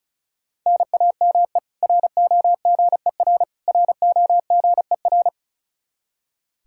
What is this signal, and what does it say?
Morse code, heard as a keyed tone, about 34 wpm, 700 Hz. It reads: NAME ROGER ROGER
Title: Practice QSO - Exchange 1 - Beginning - 35wpm